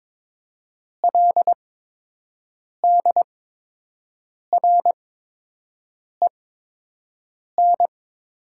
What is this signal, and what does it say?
Morse code, heard as a keyed tone, about 22 wpm, spaced as for 6 wpm, 700 Hz. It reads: LDREN
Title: 8x Character Spacing - Top 500 Words - 22wpm